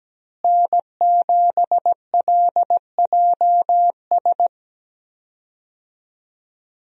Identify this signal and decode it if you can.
Morse code, heard as a keyed tone, about 17 wpm, 700 Hz. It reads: N7LJS
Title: USA Call Signs - 5 Characters - Repeated 2x - 17wpm